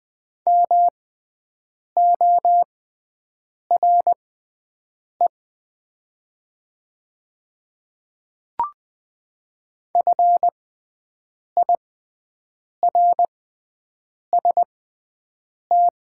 Morse code, heard as a keyed tone, about 20 wpm, spaced as for 7 wpm, 690 Hz. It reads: MORE FIRST
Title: 6x Character Spacing - Top 100 Words - 20wpm